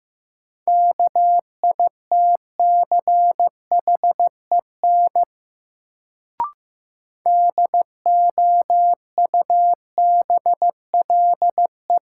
Morse code, heard as a keyed tone, about 15 wpm, 700 Hz. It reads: KITCHEN DOUBLE
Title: Sentences from Top 1400 Words - New Word Drill - 15wpm